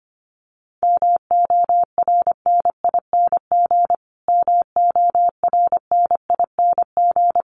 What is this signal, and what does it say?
Morse code, heard as a keyed tone, about 25 wpm, 700 Hz. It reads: MORNING MORNING
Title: Top 500 Words Repeated 25wpm